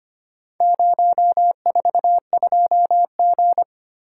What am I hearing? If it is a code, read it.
Morse code, heard as a keyed tone, about 25 wpm, 700 Hz. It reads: 042G
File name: practice / 25wpm / Four Letter-Number Combinations - 25wpm